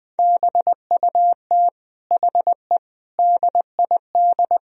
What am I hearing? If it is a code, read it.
Morse code, heard as a keyed tone, about 21 wpm, 700 Hz. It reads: BUT HE DID
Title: Sentences from Top 1000 Words - Review Part 1 - 20wpm